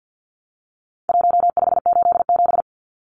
Morse code, heard as a keyed tone, about 48 wpm, 700 Hz. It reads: 1587
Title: Numbers - 4-Digit 50wpm